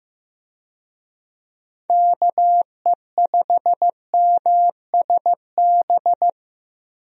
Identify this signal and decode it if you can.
Morse code, heard as a keyed tone, about 15 wpm, 700 Hz. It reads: KE5MSB
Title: USA Call Signs - 6 Characters - Repeated 3x - 15wpm